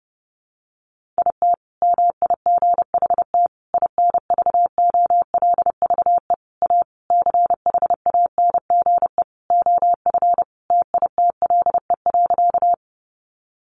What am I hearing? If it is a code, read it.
Morse code, heard as a keyed tone, about 30 wpm, 700 Hz. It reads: IT MIGHT INVOLVE A CHANGE OF TITLE.
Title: Sentences from Top 1200 Words 30wpm